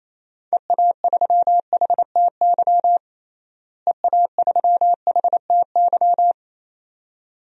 Morse code, heard as a keyed tone, about 27 wpm, 700 Hz. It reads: EA3HTY EA3HTY
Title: DXCC-100 Call Signs - 6 Characters - Repeated 2x - 28wpm